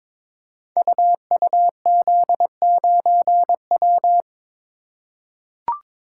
Morse code, heard as a keyed tone, about 22 wpm, 690 Hz. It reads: UUZ9W
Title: Code Groups - 5 Letter:Numbers - 22wpm